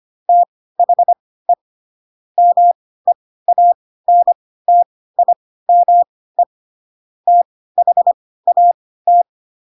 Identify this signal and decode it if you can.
Morse code, heard as a keyed tone, about 25 wpm, spaced as for 16 wpm, 700 Hz. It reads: THE MEANTIME THAT
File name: Pinocchio / Pinocchio ch4 10 25 ik1hge